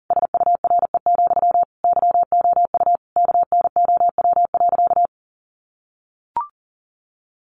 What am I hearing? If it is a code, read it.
Morse code, heard as a keyed tone, about 37 wpm, 700 Hz. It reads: SURE, YOU KNOW.
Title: Sentences from Top 700 Words 40wpm